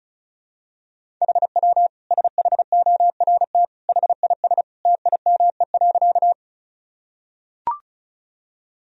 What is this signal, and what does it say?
Morse code, heard as a keyed tone, about 36 wpm, 700 Hz. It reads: HW SHORT HIS TIME.
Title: Sentences from Top 400 Words and Abbreviations 35wpm